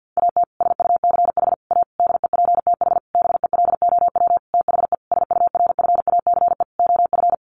Morse code, heard as a keyed tone, about 50 wpm, 700 Hz. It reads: AT SUCH A DEPTH BELOW THE SURFACE OF
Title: Sentences from Top 1200 Words 50wpm